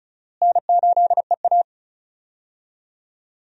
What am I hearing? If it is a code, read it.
Morse code, heard as a keyed tone, about 34 wpm, 700 Hz. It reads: N8EA
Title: USA Call Signs - 4 Characters - Repeated 3x - 35wpm